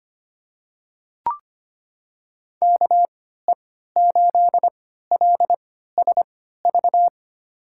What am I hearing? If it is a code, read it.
Morse code, heard as a keyed tone, about 25 wpm, spaced as for 14 wpm, 700 Hz. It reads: KE8LSV
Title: USA Call Signs - 6 Characters - 3x Character Spacing - 25wpm